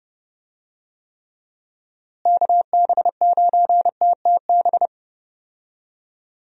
Morse code, heard as a keyed tone, about 30 wpm, 700 Hz. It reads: KB9TTB
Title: USA Call Signs - 6 Characters - Repeated 2x - 30wpm